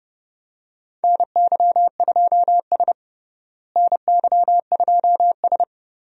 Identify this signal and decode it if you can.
Morse code, heard as a keyed tone, about 32 wpm, 700 Hz. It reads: NY2S NY2S
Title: USA Call Signs - 4 Characters - Repeated 2x - 30wpm